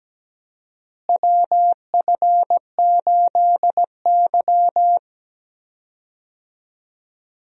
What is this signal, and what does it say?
Morse code, heard as a keyed tone, about 17 wpm, 700 Hz. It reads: WF8Y
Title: USA Call Signs - 4 Characters - Repeated 2x - 17wpm